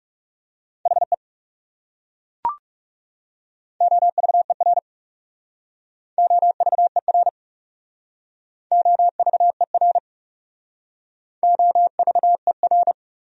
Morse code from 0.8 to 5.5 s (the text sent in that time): HE OVER